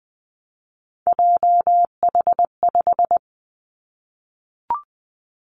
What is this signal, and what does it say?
Morse code, heard as a keyed tone, about 20 wpm, 700 Hz. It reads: JH5
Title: Three Challenging Letter-Number Combinations 20wpm